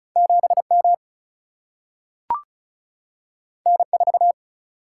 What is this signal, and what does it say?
Morse code, heard as a keyed tone, about 36 wpm, 700 Hz. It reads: 7M N4
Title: Single-Letter Single-Number 35wpm